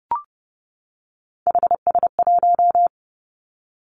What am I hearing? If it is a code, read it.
Morse code, heard as a keyed tone, about 30 wpm, 700 Hz. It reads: HS1
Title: Three Challenging Letter-Number Combinations 30wpm